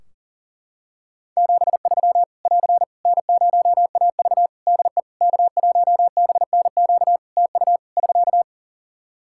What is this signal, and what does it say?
Morse code, heard as a keyed tone, about 40 wpm, 700 Hz. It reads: 73 <AR> N0AV DE K1BNQ TU <SK>
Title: Practice QSO - Exchange 4 - Ending - 40wpm